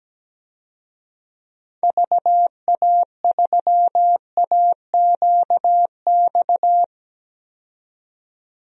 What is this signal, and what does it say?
Morse code, heard as a keyed tone, about 17 wpm, 700 Hz. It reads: VA3AQX
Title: DXCC-100 Call Signs - 6 Characters - 17wpm